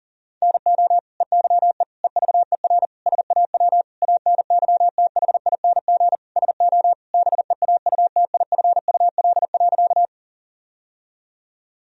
Morse code, heard as a keyed tone, about 40 wpm, 700 Hz. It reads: NO EYE EVER SAW ANYTHING SO BEAUTIFUL.
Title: Sentences from Top 1000 Words - Review Part 1 - 40wpm